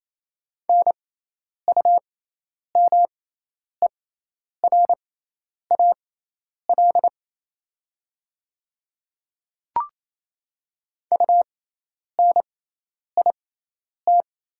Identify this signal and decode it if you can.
Morse code, heard as a keyed tone, about 28 wpm, spaced as for 10 wpm, 700 Hz. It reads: NUMERAL UNIT
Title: 6x Character Spacing - Top 500 Words - 28wpm